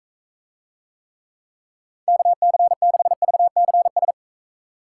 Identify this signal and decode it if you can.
Morse code, heard as a keyed tone, about 42 wpm, 700 Hz. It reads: KC6VCS